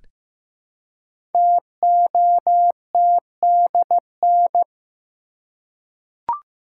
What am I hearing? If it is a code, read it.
Morse code, heard as a keyed tone, about 15 wpm, 700 Hz. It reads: TOTDN
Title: Code Groups - 5 Letters - 15wpm